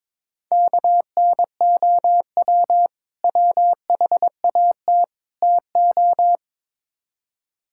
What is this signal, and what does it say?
Morse code, heard as a keyed tone, about 22 wpm, 700 Hz. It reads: KNOW WHAT TO
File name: practice / 22wpm / Build A Phrase - 5-Word Phrases - 22wpm